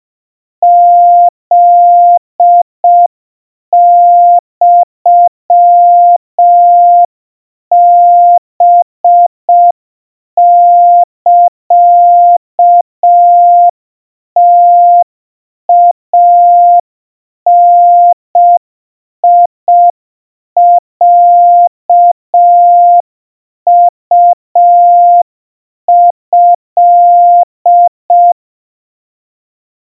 Morse code (Japanese Wabun code), heard as a keyed tone, about 5 wpm, 700 Hz. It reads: フユハサムイダロウト
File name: 20 cpm 700 Hz Crispy